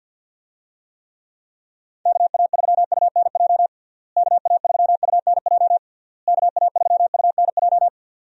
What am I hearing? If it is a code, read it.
Morse code, heard as a keyed tone, about 52 wpm, 700 Hz. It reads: KA3UNJ KA3UNJ KA3UNJ
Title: USA Call Signs - 6 Characters - Repeated 3x - 50wpm